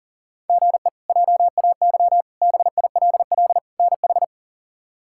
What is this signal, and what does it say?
Morse code, heard as a keyed tone, about 40 wpm, 700 Hz. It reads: GE JAY BILL NH
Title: K1USN SST QSO Practice - 40wpm